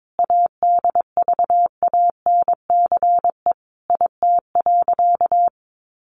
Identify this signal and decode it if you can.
Morse code, heard as a keyed tone, about 22 wpm, 700 Hz. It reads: ADVANCE IT.
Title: Sentences from Top 1400 Words 22wpm